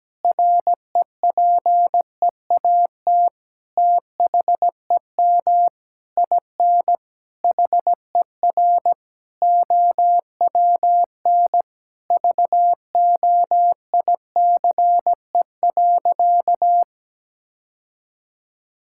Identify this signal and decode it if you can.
Morse code, heard as a keyed tone, about 17 wpm, 700 Hz. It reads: REPEAT THEM IN HER OWN VOICE.
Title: Sentences from Top 1600 Words 17wpm